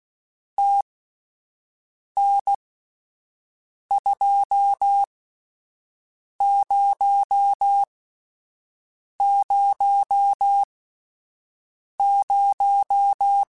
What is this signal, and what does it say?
Morse code, heard as a keyed tone, about 16 wpm, spaced as for 5 wpm, 800 Hz. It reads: TN2000
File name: (spaced) / QSOs Spaced-mp3s / WD0FBT 1 @10 sp